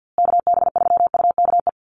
Morse code, heard as a keyed tone, about 50 wpm, 700 Hz. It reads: KB3UKE
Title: USA Call Signs Repeated 2x 50wpm